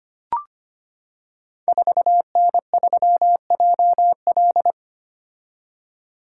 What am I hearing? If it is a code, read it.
Morse code, heard as a keyed tone, about 25 wpm, 700 Hz. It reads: 4N3JL
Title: Code Groups - 5 Letter:Numbers - 25wpm